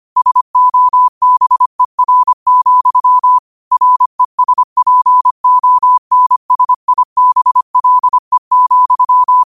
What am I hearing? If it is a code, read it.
Morse code, heard as a keyed tone, about 25 wpm, 1000 Hz. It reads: SOBER, RESPONSIBLE,
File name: qotdfiles / 25wpm